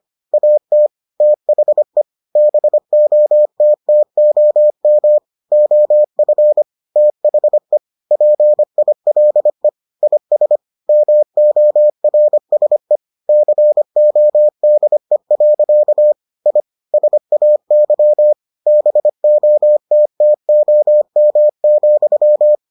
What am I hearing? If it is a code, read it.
Morse code, heard as a keyed tone, about 25 wpm, 600 Hz. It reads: AT THE BOTTOM OF THE PILE IS MORSE CODE. I SAY BOTTOM,